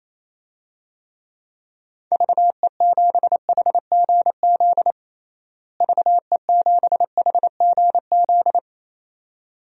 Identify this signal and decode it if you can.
Morse code, heard as a keyed tone, about 28 wpm, 700 Hz. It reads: VE7HGZ VE7HGZ
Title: DXCC-100 Call Signs - 6 Characters - Repeated 2x - 28wpm